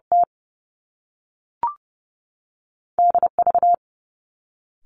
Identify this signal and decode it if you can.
Morse code, heard as a keyed tone, about 30 wpm, 700 Hz. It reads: T D V
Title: Top Two Letter Combinations in Words 30wpm